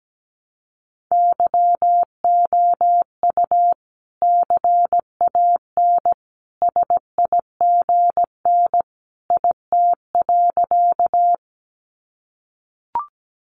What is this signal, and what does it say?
Morse code, heard as a keyed tone, about 17 wpm, 700 Hz. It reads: YOU CAN SIGN IT.
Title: Sentences from Top 1000 Words 17wpm